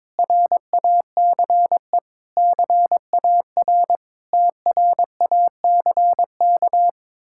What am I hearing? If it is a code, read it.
Morse code, heard as a keyed tone, about 22 wpm, 700 Hz. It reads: RACE CAR TRACK
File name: Difficult Words with Hints - 22wpm